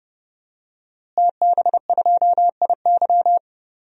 Morse code, heard as a keyed tone, about 31 wpm, 700 Hz. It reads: TB2IY
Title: DXCC-100 Call Signs - 5 Characters - Repeated 2x - 30wpm